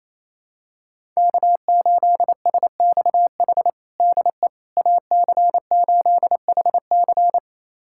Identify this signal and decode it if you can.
Morse code, heard as a keyed tone, about 28 wpm, 700 Hz. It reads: K8SXH DE AC8HC
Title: USA Callsign DE Callsign 28wpm